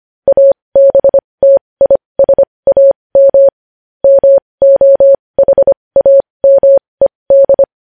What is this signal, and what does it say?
Morse code, heard as a keyed tone, about 25 wpm, spaced as for 20 wpm, 550 Hz. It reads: ABTISAM MOHAMED
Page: News Headlines in Morse Code at 20 WPM